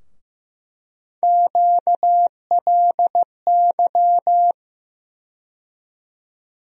Morse code, heard as a keyed tone, about 15 wpm, 700 Hz. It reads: QLY
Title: USA Call Sign Postfixes - 3 Letters 15wpm